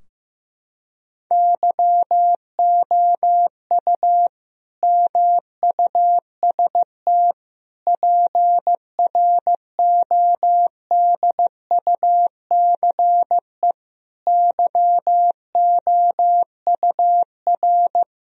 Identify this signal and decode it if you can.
Morse code, heard as a keyed tone, about 15 wpm, 700 Hz. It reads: YOU MUST PRODUCE YOUR
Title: Sentences from Top 1000 Words - Review Part 3 - 15wpm